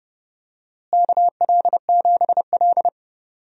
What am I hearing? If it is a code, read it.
Morse code, heard as a keyed tone, about 30 wpm, 700 Hz. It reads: KL7L
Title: DXCC-100 Call Signs - 4 Characters - Repeated 3x - 30wpm